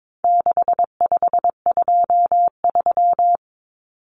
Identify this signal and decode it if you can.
Morse code, heard as a keyed tone, about 22 wpm, 700 Hz. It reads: 6523
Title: Numbers - 4-Digit 22wpm